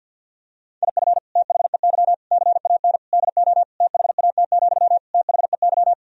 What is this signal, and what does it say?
Morse code, heard as a keyed tone, about 50 wpm, 700 Hz. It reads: IF THEY CAN DO THAT, THEY
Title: Sentences from Top 1000 Words - Review Part 4 - 50wpm